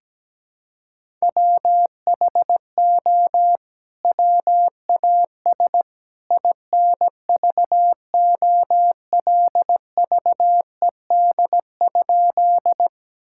Morse code, heard as a keyed tone, about 17 wpm, 700 Hz. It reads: WHO WAS INVOLVED?
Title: Sentences from Top 1000 Words - Review Part 5 - 17wpm